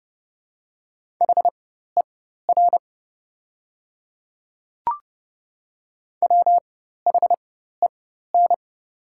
Morse code, heard as a keyed tone, about 29 wpm, spaced as for 14 wpm, 700 Hz. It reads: HER WHEN
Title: 4x Character Spacing - Top 200 Words - 30wpm